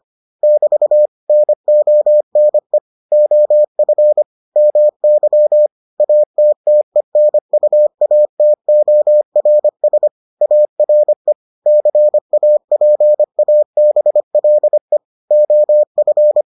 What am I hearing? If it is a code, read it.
Morse code, heard as a keyed tone, about 25 wpm, 600 Hz. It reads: <BT> NONE OF MY ATTENUATORS ARE CAPABLE OF